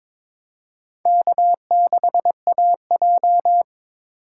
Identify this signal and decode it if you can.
Morse code, heard as a keyed tone, about 22 wpm, 700 Hz. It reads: K6AJ
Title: USA Call Signs - 4 Characters - 2x Character Spacing - 22wpm